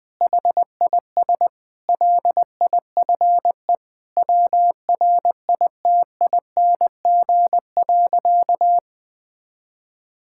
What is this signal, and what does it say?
Morse code, heard as a keyed tone, about 20 wpm, 700 Hz. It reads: HIS LIFE WRITING.
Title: Sentences with 5-Word Phrases - 20wpm